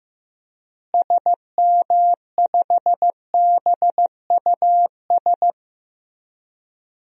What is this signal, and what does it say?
Morse code, heard as a keyed tone, about 15 wpm, 700 Hz. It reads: SM5BUS
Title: DXCC-100 Call Signs - 6 Characters - Repeated 2x - 15wpm